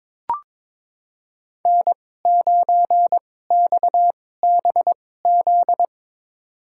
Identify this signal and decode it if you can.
Morse code, heard as a keyed tone, about 22 wpm, spaced as for 16 wpm, 700 Hz. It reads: N9XBZ
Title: USA Call Signs - 2x Character Spacing - 22wpm